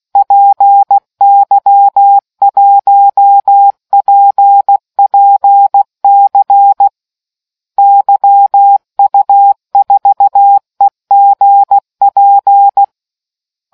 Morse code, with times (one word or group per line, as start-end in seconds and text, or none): 0.2-6.9 PY1PPC
7.8-12.8 YU4EGP